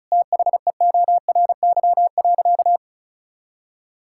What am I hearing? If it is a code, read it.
Morse code, heard as a keyed tone, about 36 wpm, 700 Hz. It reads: THEORY.
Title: Sentences from Top 1000 Words - Review Part 5 - 35wpm